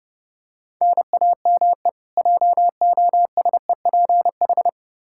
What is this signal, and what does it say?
Morse code, heard as a keyed tone, about 32 wpm, 700 Hz. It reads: NAME JOSEPH
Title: Practice QSO - Exchange 1 - Beginning - 30wpm